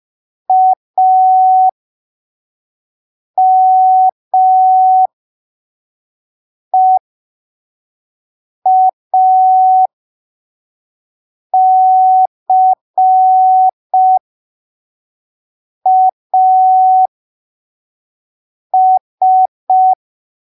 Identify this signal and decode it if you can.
Morse code, heard as a keyed tone, about 5 wpm, spaced as for 3 wpm, 750 Hz. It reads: AMEACAS
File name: arquivos / texto / texto 30 05ppm